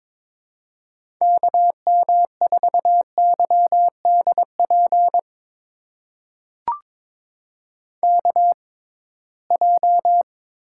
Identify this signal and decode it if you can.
Morse code, heard as a keyed tone, about 22 wpm, spaced as for 17 wpm, 700 Hz. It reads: KM4YDP KJ
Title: USA Call Signs - 6 Characters - 6x Character Spacing - 22wpm